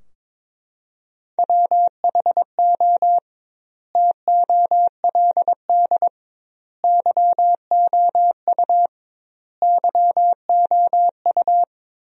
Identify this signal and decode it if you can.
Morse code, heard as a keyed tone, about 22 wpm, 700 Hz. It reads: WHO TOLD YOU YOU
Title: 2x Word-Spacing - Sentences from Top 400 Words - 22wpm